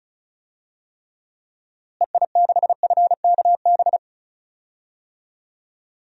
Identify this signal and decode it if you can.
Morse code, heard as a keyed tone, about 36 wpm, 700 Hz. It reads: EI6FKB
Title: DXCC-100 Call Signs - 6 Characters - Repeated 2x - 35wpm